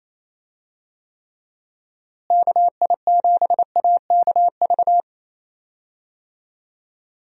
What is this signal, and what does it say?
Morse code, heard as a keyed tone, about 28 wpm, 700 Hz. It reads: KI7AKV
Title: USA Call Signs - 6 Characters - Repeated 2x - 28wpm